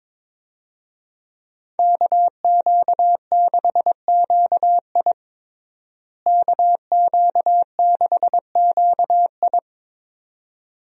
Morse code, heard as a keyed tone, about 22 wpm, 700 Hz. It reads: KQ6QI KQ6QI
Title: USA Call Signs - 5 Characters - Repeated 2x - 22wpm